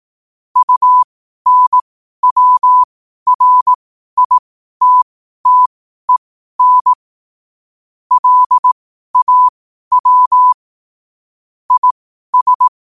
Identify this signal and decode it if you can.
Morse code, heard as a keyed tone, about 18 wpm, spaced as for 12 wpm, 1000 Hz. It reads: UNWRITTEN LAW IS
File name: qotdfiles / 12wpm